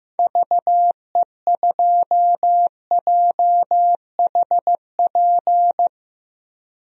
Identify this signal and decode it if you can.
Morse code, heard as a keyed tone, about 15 wpm, 700 Hz. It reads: VE2JHP
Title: DXCC-100 Call Signs - 6 Characters - 15wpm